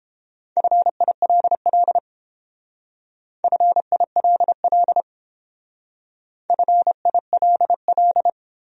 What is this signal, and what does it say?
Morse code, heard as a keyed tone, about 30 wpm, 700 Hz. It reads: FILL FILL FILL